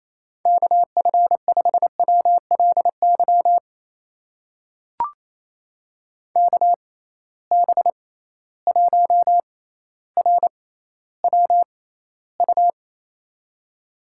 Morse code, heard as a keyed tone, about 28 wpm, spaced as for 14 wpm, 700 Hz. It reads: KF5WLY KB1RWU
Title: USA Call Signs - 6 Characters - 6x Character Spacing - 28wpm